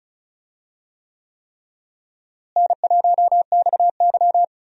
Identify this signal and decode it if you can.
Morse code, heard as a keyed tone, about 36 wpm, 700 Hz. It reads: N1XY
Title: USA Call Signs - 4 Characters - Repeated 3x - 35wpm